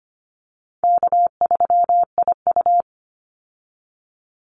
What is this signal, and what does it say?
Morse code, heard as a keyed tone, about 25 wpm, 700 Hz. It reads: K3IU
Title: Easy Call Signs Repeated 3x - 25wpm